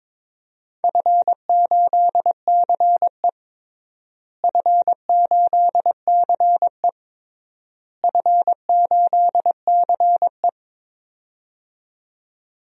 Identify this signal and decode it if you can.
Morse code, heard as a keyed tone, about 22 wpm, 700 Hz. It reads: F8CE F8CE F8CE
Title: DXCC-100 Call Signs - 4 Characters - Repeated 3x - 22wpm